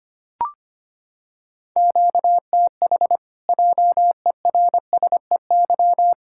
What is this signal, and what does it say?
Morse code, heard as a keyed tone, about 25 wpm, 700 Hz. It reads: QTH JERSEY
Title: Practice QSO - Exchange 1 - Beginning - 25wpm